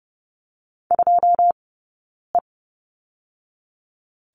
Morse code, heard as a keyed tone, about 30 wpm, 700 Hz. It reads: 2 E
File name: Lesson 18d - Letter m - Two Characters 30wpm